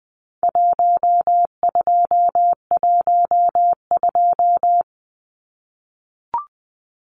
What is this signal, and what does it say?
Morse code, heard as a keyed tone, about 20 wpm, 700 Hz. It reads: 1212
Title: Numbers - Contest Style - 4-Digit 20wpm